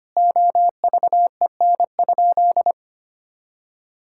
Morse code, heard as a keyed tone, about 25 wpm, 700 Hz. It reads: OVEN?
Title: Sentences from Top 1000 Words - Review Part 6 - 25wpm